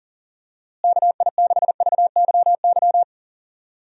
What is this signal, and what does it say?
Morse code, heard as a keyed tone, about 39 wpm, 700 Hz. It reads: KI6VYY
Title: USA Call Signs - 6 Characters - Repeated 3x - 40wpm